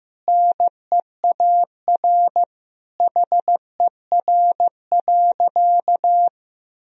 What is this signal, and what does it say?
Morse code, heard as a keyed tone, about 16 wpm, 700 Hz. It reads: NEAR HER.
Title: Sentences from Top 1000 Words - Review Part 1 - 15wpm